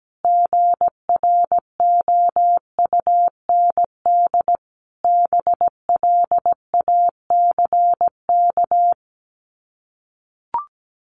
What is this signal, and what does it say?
Morse code, heard as a keyed tone, about 17 wpm, 700 Hz. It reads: GROUND BLACK
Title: Sets of 5 Words Encore 17wpm